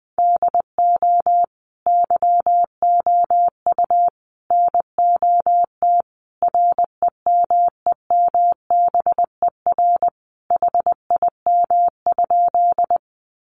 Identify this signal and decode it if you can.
Morse code, heard as a keyed tone, about 20 wpm, 700 Hz. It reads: DO YOU NOT REMEMBER HIM?
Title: Sentences from Top 600 Words 20wpm